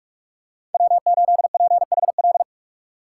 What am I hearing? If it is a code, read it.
Morse code, heard as a keyed tone, about 46 wpm, 700 Hz. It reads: W8PHL
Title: USA Call Signs - 5 Characters - 3x Character Spacing - 45wpm